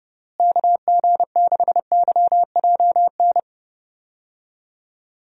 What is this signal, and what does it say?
Morse code, heard as a keyed tone, about 32 wpm, 700 Hz. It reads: KG6YJN
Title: USA Call Signs - 6 Characters - Repeated 2x - 30wpm